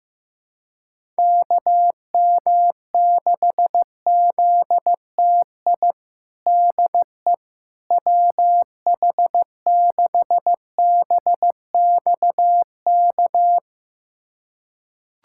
Morse code, heard as a keyed tone, about 16 wpm, 700 Hz. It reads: KM6ZTI DE WH6BXK
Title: USA Callsign DE Callsign - 6 Characters - 15wpm